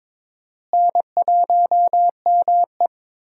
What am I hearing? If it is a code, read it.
Morse code, heard as a keyed tone, about 22 wpm, 700 Hz. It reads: N1ME
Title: USA Call Signs - 4 Characters - Repeated 3x - 22wpm